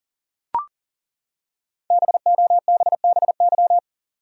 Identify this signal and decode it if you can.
Morse code, heard as a keyed tone, about 40 wpm, 700 Hz. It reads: BOBBY
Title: USA Common Names - 40wpm